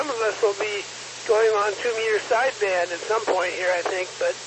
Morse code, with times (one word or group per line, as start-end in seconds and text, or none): none